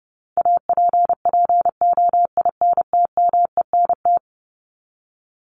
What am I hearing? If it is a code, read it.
Morse code, heard as a keyed tone, about 30 wpm, 700 Hz. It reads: APPOINTMENT
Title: Top 1751-2000 Words 30wpm